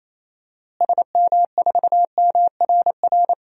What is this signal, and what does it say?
Morse code, heard as a keyed tone, about 28 wpm, 700 Hz. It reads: SM4MRR
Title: DXCC-100 Call Signs - 6 Characters - Repeated 2x - 28wpm